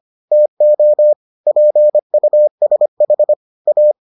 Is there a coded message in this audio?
Yes: Morse code from 0.3 to 3.9 s, heard as a keyed tone, about 25 wpm, 600 Hz. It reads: TO PUSH A